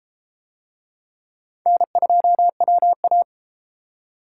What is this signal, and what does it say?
Morse code, heard as a keyed tone, about 32 wpm, 700 Hz. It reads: N2WA